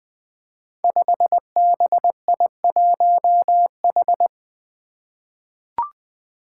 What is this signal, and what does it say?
Morse code, heard as a keyed tone, about 20 wpm, 700 Hz. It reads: 5BI1H
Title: Five Challenging Letter-Number Combinations 20wpm